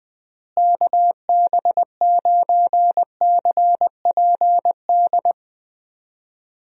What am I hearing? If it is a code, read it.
Morse code, heard as a keyed tone, about 20 wpm, 700 Hz. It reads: KB9CPD
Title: USA Call Signs - 6 Characters - Repeated 3x - 20wpm